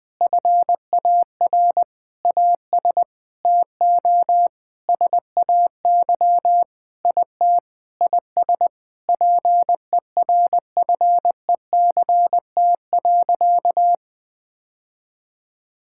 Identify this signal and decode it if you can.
Morse code, heard as a keyed tone, about 20 wpm, 700 Hz. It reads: FAR AS TO SAY IT IS PERFECT.
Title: Sentences with 8-Word Phrases - 20wpm